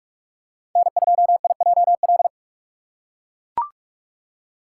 Morse code, heard as a keyed tone, about 42 wpm, 700 Hz. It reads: N2IJL
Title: USA Call Signs - 5 Characters - 3x Character Spacing - 45wpm